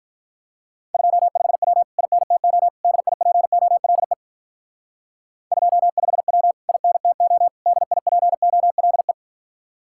Morse code, heard as a keyed tone, about 45 wpm, 700 Hz. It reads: 25W INTO DIPOLE 25W INTO DIPOLE